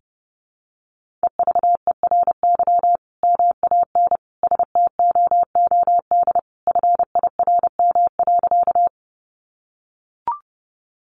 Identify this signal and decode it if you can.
Morse code, heard as a keyed tone, about 30 wpm, 700 Hz. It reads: EVERY MAN STOOD FIRM.
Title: Sentences from Top 1400 Words 30wpm